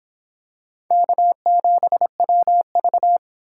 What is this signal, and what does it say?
Morse code, heard as a keyed tone, about 23 wpm, 700 Hz. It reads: K7WV